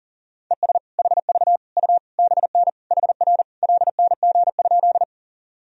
Morse code, heard as a keyed tone, about 37 wpm, 700 Hz. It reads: ES HV U BN HR LNG?
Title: Sentences from Top 200 Words and Abbreviations 40wpm